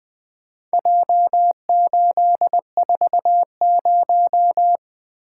Morse code, heard as a keyed tone, about 20 wpm, 700 Hz. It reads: J840
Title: Four Letter-Number Combinations - 20wpm